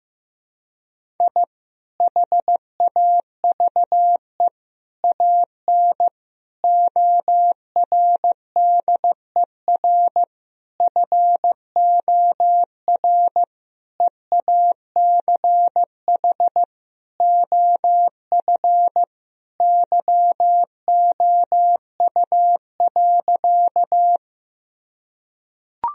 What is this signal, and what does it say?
Morse code, heard as a keyed tone, about 15 wpm, 700 Hz. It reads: I HAVE AN ORDER FOR EACH OF YOU.
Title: Sentences from Top 1000 Words - Review Part 2 - 15wpm